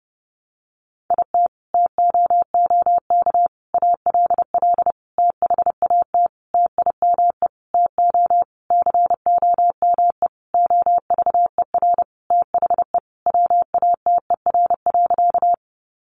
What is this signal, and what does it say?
Morse code, heard as a keyed tone, about 30 wpm, 700 Hz. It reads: IT TOOK ALL THAT TIME TO COME OVER THE WATER.
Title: Sentences from Top 300 Words 30wpm